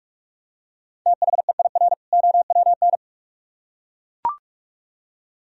Morse code, heard as a keyed tone, about 41 wpm, 700 Hz. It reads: THEIR OWN